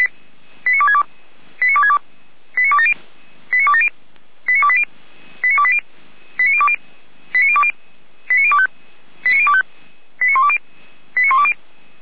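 Five-tone selective call (ZVEI), heard as quick runs of five tones, tones of about 70 ms. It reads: 89262 89262 89270 89270 89280 89280 89020 89020 89025 89025 89120 89120
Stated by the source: ZVEI five-tone call 89358 received by SDR.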